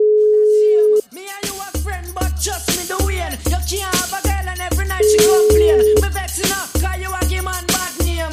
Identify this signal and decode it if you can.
Ringback tone (Europe, 425 Hz), with noise.